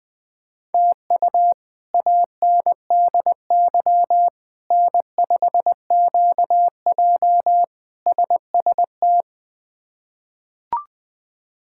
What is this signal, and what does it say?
Morse code, heard as a keyed tone, about 20 wpm, 700 Hz. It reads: TU ANDY N5QJ SST
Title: K1USN SST QSO Practice - 20wpm